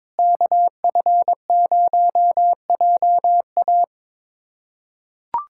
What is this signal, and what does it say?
Morse code, heard as a keyed tone, about 22 wpm, 700 Hz. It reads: KF0JA
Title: USA Callsign DE Callsign - 5 Characters - 22wpm